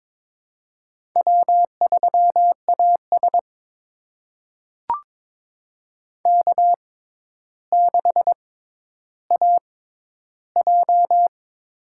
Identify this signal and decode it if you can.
Morse code, heard as a keyed tone, about 22 wpm, spaced as for 11 wpm, 700 Hz. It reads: W3AS K6AJ
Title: USA Call Signs - 4 Characters - 6x Character Spacing - 22wpm